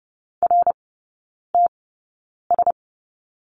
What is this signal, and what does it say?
Morse code, heard as a keyed tone, about 30 wpm, spaced as for 9 wpm, 700 Hz. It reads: RTS
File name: 081 Lesson 14e - Number 2 - Three Characters 30wpm